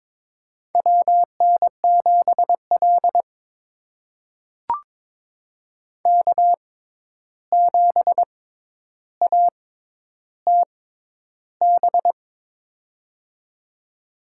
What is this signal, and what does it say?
Morse code, heard as a keyed tone, about 22 wpm, spaced as for 11 wpm, 700 Hz. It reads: WN7L K7ATB
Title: USA Call Signs - 6x Character Spacing - 22wpm